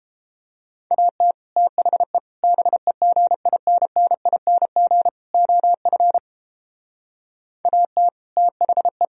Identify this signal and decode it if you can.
Morse code, heard as a keyed tone, about 29 wpm, 700 Hz. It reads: AT THE BEGINNING OF AT THE